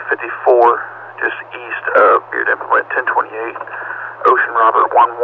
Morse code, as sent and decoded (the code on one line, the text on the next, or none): none